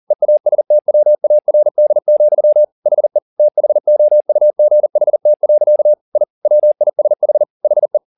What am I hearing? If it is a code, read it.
Morse code, heard as a keyed tone, about 40 wpm, 600 Hz. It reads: EASTWARD, HE THOUGHT. I WISH HE